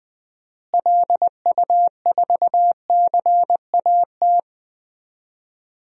Morse code, heard as a keyed tone, about 19 wpm, 700 Hz. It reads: LU4CAT
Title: DXCC-100 Call Signs - 6 Characters - Repeated 2x - 20wpm